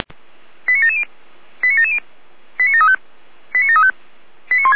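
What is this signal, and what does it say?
Five-tone selective call (ZVEI), heard as quick runs of five tones, tones of about 70 ms. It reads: 89770 89770 89735 89735
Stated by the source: ZVEI five-tone selective call received by SDR.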